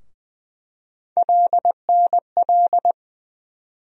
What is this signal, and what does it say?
Morse code, heard as a keyed tone, about 20 wpm, 700 Hz. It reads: LNL
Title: USA Call Sign Postfixes - 3 Letters 20wpm